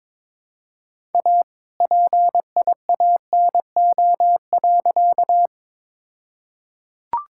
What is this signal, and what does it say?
Morse code, heard as a keyed tone, about 22 wpm, 700 Hz. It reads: A PIANO.
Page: morse-code-ninja-practice-22wpm